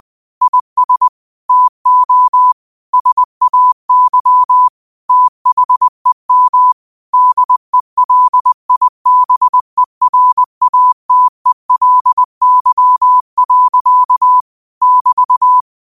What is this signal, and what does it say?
Morse code, heard as a keyed tone, about 20 wpm, 1000 Hz. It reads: IS TO SAY THEM DELIBERATELY. <BT>